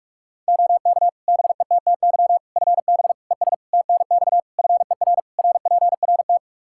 Morse code, heard as a keyed tone, about 43 wpm, 700 Hz. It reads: OK BETTY FB ES TNX FER RPRT
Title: Practice QSO - Exchange 2 - Info - 45wpm